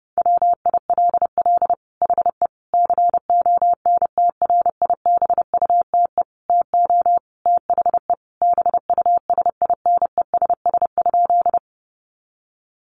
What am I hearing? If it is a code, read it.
Morse code, heard as a keyed tone, about 30 wpm, 700 Hz. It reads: WILL HE CONTRIBUTE TO THE BUSINESS?
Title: Sentences from Top 1400 Words 30wpm